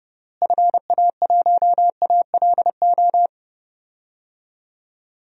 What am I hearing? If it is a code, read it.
Morse code, heard as a keyed tone, about 32 wpm, 700 Hz. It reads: FA1ALO
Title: DXCC-100 Call Signs - 6 Characters - Repeated 2x - 30wpm